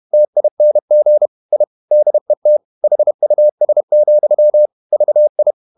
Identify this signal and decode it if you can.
Morse code, heard as a keyed tone, about 31 wpm, 600 Hz. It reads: TING I DET HUS, VI
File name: Ferie QRQ 31wpm